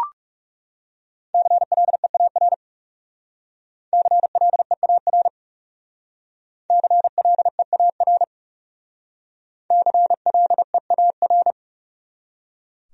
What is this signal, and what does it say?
Morse code, heard as a keyed tone, about 36 wpm, 700 Hz. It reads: CLEAR CLEAR CLEAR CLEAR
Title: Speed-Racing Top 500 Words 30wpm